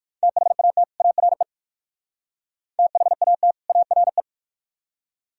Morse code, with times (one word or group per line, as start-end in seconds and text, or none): 0.2-0.8 THAT
1.0-1.4 ARE
2.8-3.5 THAT
3.7-4.2 ARE